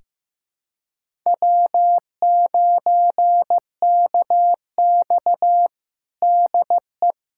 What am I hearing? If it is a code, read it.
Morse code, heard as a keyed tone, about 15 wpm, 700 Hz. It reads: W9KX DE
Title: USA Callsign DE Callsign - 4 Characters - 15wpm